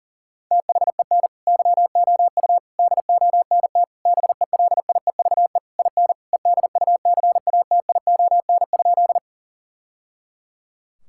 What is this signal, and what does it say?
Morse code, heard as a keyed tone, about 40 wpm, 700 Hz. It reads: THEN YOU DONT BELIEVE IN EDUCATION?
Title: Sentences from Top 1000 Words - Review Part 6 - 40wpm